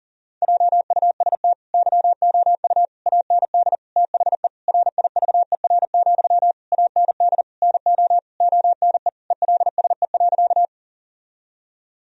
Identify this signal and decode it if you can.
Morse code, heard as a keyed tone, about 37 wpm, 700 Hz. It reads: JUST YOU AND THE RIVER, AND NO ONE ELSE.
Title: Sentences from Top 1000 Words - Review Part 1 - 40wpm